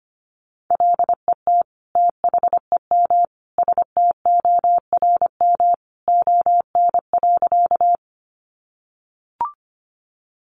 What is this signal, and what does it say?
Morse code, heard as a keyed tone, about 25 wpm, 700 Hz. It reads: LET THEM STORM ON.
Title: Sentences from Top 1500 Words 25wpm